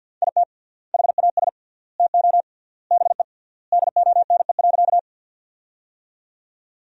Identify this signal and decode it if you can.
Morse code, heard as a keyed tone, about 47 wpm, 700 Hz. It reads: IT HAS TO BE DONE.
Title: 3x Word-Spacing - Sentences from Top 500 Words - 50wpm